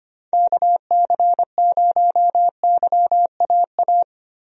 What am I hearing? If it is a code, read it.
Morse code, heard as a keyed tone, about 25 wpm, 700 Hz. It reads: KC0YAA
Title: USA Call Signs - 6 Characters - Repeated 3x - 25wpm